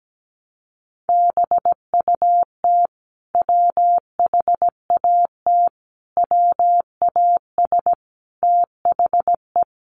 Morse code, heard as a keyed tone, about 17 wpm, 700 Hz. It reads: BUT WHAT WAS THE
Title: Sentences from Top 1100 Words 17wpm